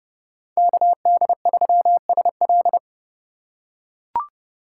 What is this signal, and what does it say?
Morse code, heard as a keyed tone, about 30 wpm, 700 Hz. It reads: KD3SL
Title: USA Call Signs - 5 Characters - 2x Character Spacing - 30wpm